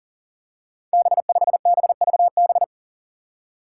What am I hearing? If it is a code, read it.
Morse code, heard as a keyed tone, about 40 wpm, 700 Hz. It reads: B5BVB
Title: Five Challenging Letter-Number Combinations 40wpm